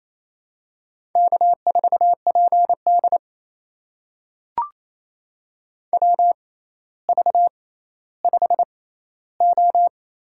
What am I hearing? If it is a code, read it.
Morse code, heard as a keyed tone, about 28 wpm, spaced as for 14 wpm, 700 Hz. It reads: K4PD WV5O
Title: USA Call Signs - 4 Characters - 6x Character Spacing - 28wpm